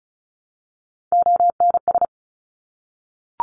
Morse code, heard as a keyed tone, about 36 wpm, 700 Hz. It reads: ONS
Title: Top Three Letter Combinations in Words 35wpm